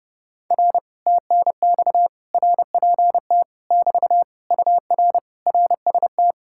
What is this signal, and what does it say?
Morse code, heard as a keyed tone, about 28 wpm, 700 Hz. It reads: R TNX RPT <BT> UR RST
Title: Speed-Racing Common QSO Phrases Encore - 30wpm